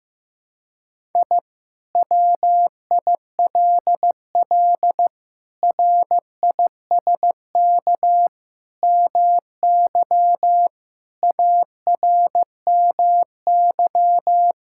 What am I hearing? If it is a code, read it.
Morse code, heard as a keyed tone, about 16 wpm, 700 Hz. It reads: I WILL RISK MY ARMY
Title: Sentences from Top 1000 Words - Review Part 5 - 15wpm